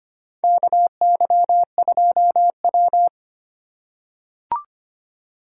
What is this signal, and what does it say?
Morse code, heard as a keyed tone, about 25 wpm, 700 Hz. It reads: KY2W
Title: USA Call Signs - 4 Characters - 3x Character Spacing - 25wpm